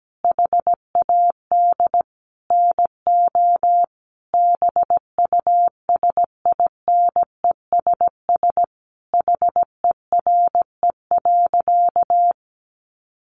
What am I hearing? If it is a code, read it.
Morse code, heard as a keyed tone, about 17 wpm, 700 Hz. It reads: HAD NO BUSINESS HERE.
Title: Sentences from Top 1300 Words 17wpm